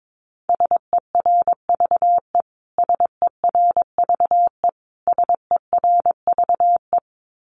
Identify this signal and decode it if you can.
Morse code, heard as a keyed tone, about 22 wpm, 700 Hz. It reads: SERVE SERVE SERVE
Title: Top 500 Words Repeated 22wpm